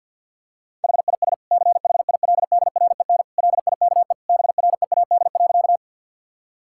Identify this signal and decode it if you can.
Morse code, heard as a keyed tone, about 50 wpm, 700 Hz. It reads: HIS CHILDREN LIKE BREAD.
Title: Sentences from Top 1600 Words 50wpm